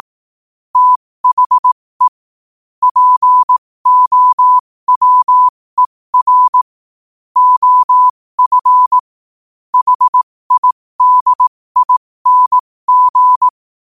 Morse code, heard as a keyed tone, about 18 wpm, spaced as for 15 wpm, 1000 Hz. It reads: THE POWER OF HIDING
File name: qotdfiles / 15wpm